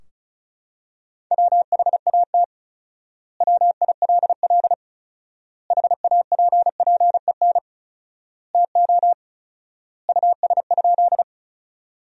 Morse code, heard as a keyed tone, about 36 wpm, 700 Hz. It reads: WHAT WILL HAPPEN TO US?
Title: 4x Word-Spacing - Sentences from Top 400 Words - 35wpm